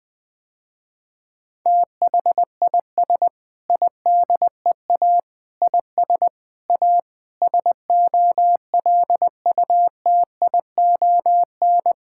Morse code, heard as a keyed tone, about 21 wpm, 700 Hz. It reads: THIS IDEA IS A SOLUTION
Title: Sentences with 8-Word Phrases - Spoken Beginning - 20wpm